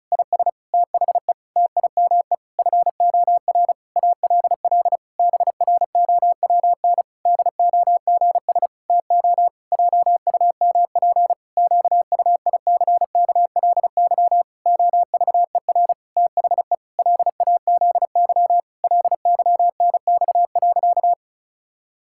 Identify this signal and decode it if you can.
Morse code, heard as a keyed tone, about 36 wpm, 700 Hz. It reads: IS THE TIME FOR ALL BROWN DOGS TO JUMP QUICKLY OVER THE LAZY LYNX.
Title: Pangram Challenge v2 - 35wpm